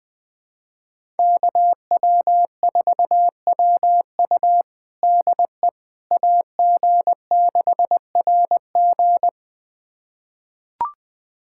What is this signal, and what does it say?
Morse code, heard as a keyed tone, about 20 wpm, 700 Hz. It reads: KW4WU DE AG6RG
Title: USA Callsign DE Callsign - 5 Characters - 20wpm